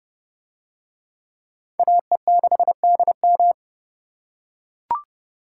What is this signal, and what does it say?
Morse code, heard as a keyed tone, about 30 wpm, 700 Hz. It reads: AE6DM